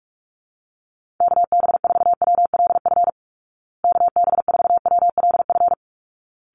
Morse code, heard as a keyed tone, about 44 wpm, 700 Hz. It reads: KB4WLF KB4WLF
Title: Call Signs Repeated 3x - 45wpm